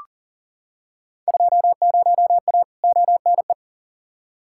Morse code, heard as a keyed tone, about 38 wpm, 700 Hz. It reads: 20A ONE
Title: Field Day Exchanges Encore - 40wpm